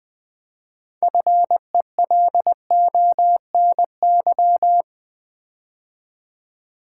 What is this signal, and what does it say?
Morse code, heard as a keyed tone, about 20 wpm, 690 Hz. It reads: FELONY
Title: 6 Letter Words - First 3 Letters Given - 20wpm